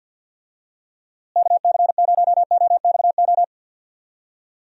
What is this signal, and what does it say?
Morse code, heard as a keyed tone, about 49 wpm, 700 Hz. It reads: KC0OXO
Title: USA Call Signs - 6 Characters - Repeated 2x - 50wpm